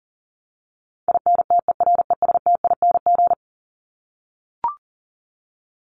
Morse code, heard as a keyed tone, about 40 wpm, 700 Hz. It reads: INTERESTING
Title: Top 751-1000 Words 40wpm